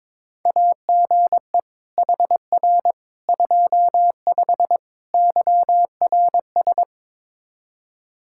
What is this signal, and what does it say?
Morse code, heard as a keyed tone, about 21 wpm, 700 Hz. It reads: AGE HR 25 YRS
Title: K1USN SST QSO Practice 22wpm